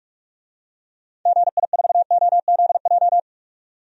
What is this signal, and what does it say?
Morse code, heard as a keyed tone, about 42 wpm, 700 Hz. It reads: GI4OZJ
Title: DXCC-100 Call Signs - 6 Characters - 45wpm